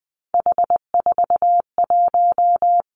Morse code, heard as a keyed tone, about 20 wpm, 700 Hz. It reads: H41
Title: Three Challenging Letter-Number Combinations 20wpm